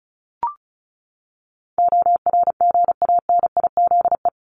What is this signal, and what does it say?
Morse code, heard as a keyed tone, about 34 wpm, 700 Hz. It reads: ORGANIZE
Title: Top 1751-2000 Words 35wpm